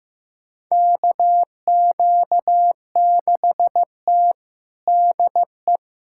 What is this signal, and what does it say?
Morse code, heard as a keyed tone, about 15 wpm, 700 Hz. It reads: KQ6T DE
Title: USA Callsign DE Callsign - 4 Characters - 15wpm